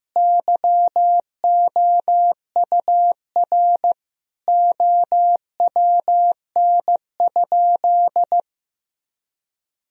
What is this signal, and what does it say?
Morse code, heard as a keyed tone, about 16 wpm, 700 Hz. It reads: YOUR OWN?
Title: Sentences from Top 1000 Words - Review Part 4 - 15wpm